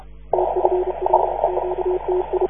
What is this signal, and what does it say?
Morse code, heard as a keyed tone, about 30 wpm, 370 Hz, with radio noise. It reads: TFA QTI
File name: 1esempio-telegrafia